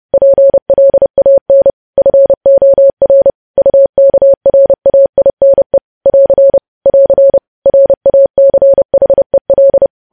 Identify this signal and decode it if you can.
Morse code, heard as a keyed tone, about 30 wpm, 550 Hz. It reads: PLAN FOR UKRAINE <AR> <AR> RACHEL
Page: news headlines in morse code at 30 wpm